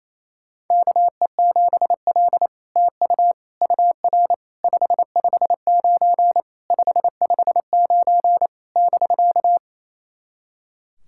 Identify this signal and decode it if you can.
Morse code, heard as a keyed tone, about 28 wpm, 700 Hz. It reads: KE7L TU UR 559 559 <BK>
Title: SOTA QSO Practice 28wpm